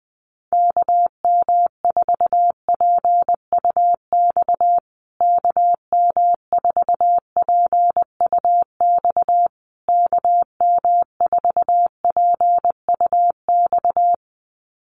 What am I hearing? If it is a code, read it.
Morse code, heard as a keyed tone, about 20 wpm, 700 Hz. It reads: KM4PUX KM4PUX KM4PUX
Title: USA Call Signs Repeated 3x 20wpm